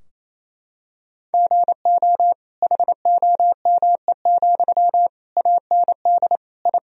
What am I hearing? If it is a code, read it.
Morse code, heard as a keyed tone, about 27 wpm, 700 Hz. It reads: GO HOME, AND I
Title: Sentences from Top 1000 Words - Review Part 4 - 28wpm